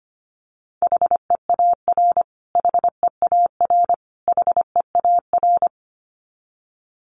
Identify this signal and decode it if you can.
Morse code, heard as a keyed tone, about 25 wpm, 700 Hz. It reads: HEAR HEAR HEAR
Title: Top 500 Words Repeated 25wpm